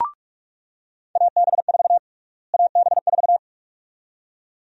Morse code, heard as a keyed tone, about 46 wpm, 690 Hz. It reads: AB4 AB4
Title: USA Call Sign Prefixes - Repeated 2x - 45wpm